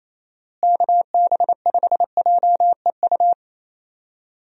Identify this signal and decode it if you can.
Morse code, heard as a keyed tone, about 28 wpm, 700 Hz. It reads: KB5JEU
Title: USA Call Signs - 6 Characters - Repeated 2x - 28wpm